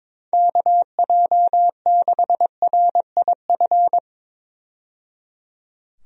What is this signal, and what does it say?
Morse code, heard as a keyed tone, about 22 wpm, 700 Hz. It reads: KJ6RIF
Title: USA Call Signs - 6 Characters - 22wpm